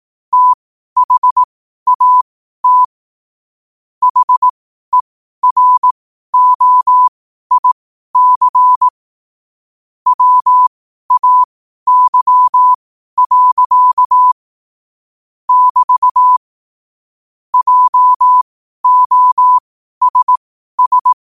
Morse code, heard as a keyed tone, about 18 wpm, spaced as for 12 wpm, 1000 Hz. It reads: THAT HEROIC WAY. <BT> JOSS
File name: qotdfiles / 12wpm